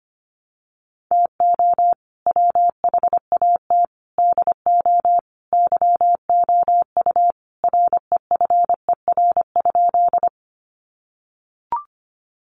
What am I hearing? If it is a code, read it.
Morse code, heard as a keyed tone, about 25 wpm, 700 Hz. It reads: TO WHAT DO YOU REFER?
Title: Sentences from Top 1000 Words 25wpm